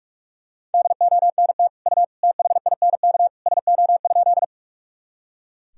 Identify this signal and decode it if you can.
Morse code, heard as a keyed tone, about 41 wpm, 700 Hz. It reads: DONT U THINK SO?
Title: Sentences from Top 300 Words and Abbreviations 45wpm